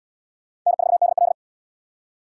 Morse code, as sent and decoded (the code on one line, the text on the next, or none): -. ....- -.. .-..
N4DL